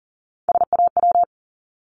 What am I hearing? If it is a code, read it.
Morse code, heard as a keyed tone, about 38 wpm, 690 Hz. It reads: SAW
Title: Multi-letter Phonemes 40wpm